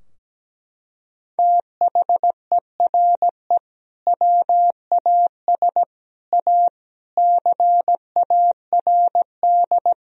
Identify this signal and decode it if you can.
Morse code, heard as a keyed tone, about 17 wpm, 700 Hz. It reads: THERE WAS A CARD
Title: Sentences from Top 1000 Words - Review Part 4 - 17wpm